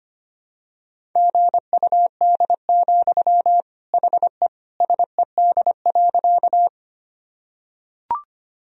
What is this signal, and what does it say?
Morse code, heard as a keyed tone, about 25 wpm, 700 Hz. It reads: GUD, HE SED.
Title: Sentences from Top 200 Words and Abbreviations 25wpm